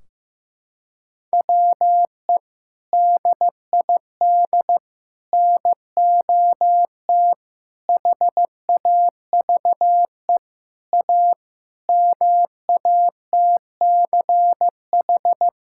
Morse code, heard as a keyed tone, about 16 wpm, 700 Hz. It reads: WE DID NOT HAVE A MATCH
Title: Sentences from Top 1600 Words 15wpm